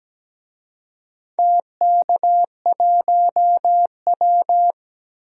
Morse code, heard as a keyed tone, about 17 wpm, 700 Hz. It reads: TK1W
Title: DXCC-100 Call Signs - 4 Characters - 17wpm